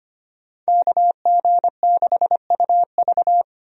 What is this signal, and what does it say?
Morse code, heard as a keyed tone, about 25 wpm, 700 Hz. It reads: KG6UV
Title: USA Call Signs - 5 Characters - 25wpm